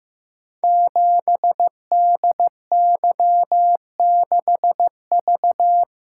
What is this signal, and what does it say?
Morse code, heard as a keyed tone, about 15 wpm, 700 Hz. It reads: 7DY6V
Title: Code Groups - 5 Letter:Numbers - 15wpm